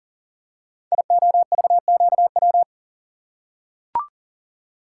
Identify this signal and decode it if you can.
Morse code, heard as a keyed tone, about 41 wpm, 700 Hz. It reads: IOVQW
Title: Code Groups - 5 Letter:Numbers - 40wpm